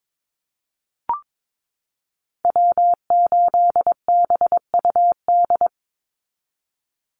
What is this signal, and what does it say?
Morse code, heard as a keyed tone, about 22 wpm, 700 Hz. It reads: W8BUD
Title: Call Signs Repeated 2x - 22wpm